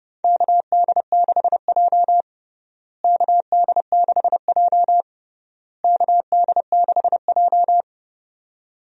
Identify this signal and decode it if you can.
Morse code, heard as a keyed tone, about 32 wpm, 700 Hz. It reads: KD6J KD6J KD6J
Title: USA Call Signs - 4 Characters - Repeated 3x - 30wpm